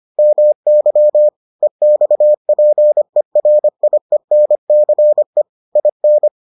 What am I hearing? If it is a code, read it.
Morse code, heard as a keyed tone, about 25 wpm, 600 Hz. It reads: MY EXPERIENCE IN